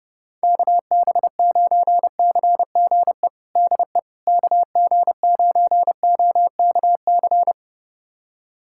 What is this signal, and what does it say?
Morse code, heard as a keyed tone, about 30 wpm, 700 Hz. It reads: KB9CGE DE KG9OKC
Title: USA Callsign DE Callsign - 6 Characters - 30wpm